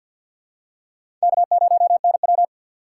Morse code, heard as a keyed tone, about 49 wpm, 700 Hz. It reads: K0NW
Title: K1USN SST QSO Practice - 50wpm